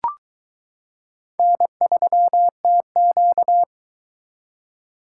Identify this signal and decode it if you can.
Morse code, heard as a keyed tone, about 22 wpm, 700 Hz. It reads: N3TQ